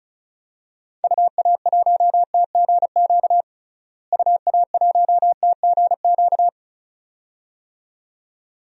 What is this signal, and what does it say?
Morse code, heard as a keyed tone, about 36 wpm, 700 Hz. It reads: UA1TGQ UA1TGQ
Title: DXCC-100 Call Signs - 6 Characters - Repeated 2x - 35wpm